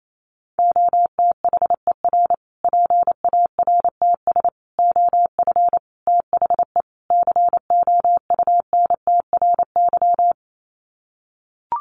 Morse code, heard as a keyed tone, about 28 wpm, 700 Hz. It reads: OTHER PARTS OF THE COUNTRY
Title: Sets of 5 Words Encore 28wpm